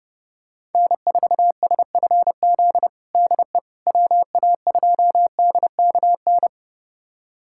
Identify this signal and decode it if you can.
Morse code, heard as a keyed tone, about 30 wpm, 700 Hz. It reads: N4SFZ DE WA2DKN
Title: USA Callsign DE Callsign 30wpm